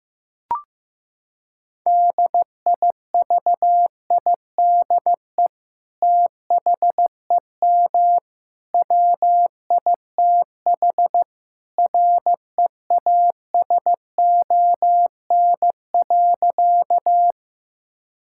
Morse code, heard as a keyed tone, about 15 wpm, 700 Hz. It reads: DIVIDE THEM WITH REASON.
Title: Sentences from Top 1600 Words 15wpm